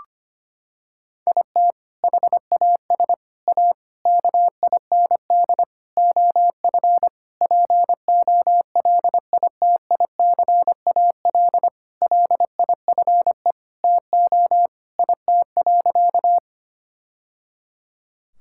Morse code, heard as a keyed tone, about 25 wpm, 700 Hz. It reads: IT HAS A KIND OF POLITICAL LIFE TO IT.
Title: Sentences from Top 1000 Words - Review Part 5 - 25wpm